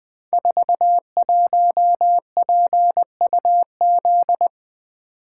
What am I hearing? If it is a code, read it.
Morse code, heard as a keyed tone, about 20 wpm, 700 Hz. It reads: 41PUZ
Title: Code Groups - 5 Letter:Numbers - 20wpm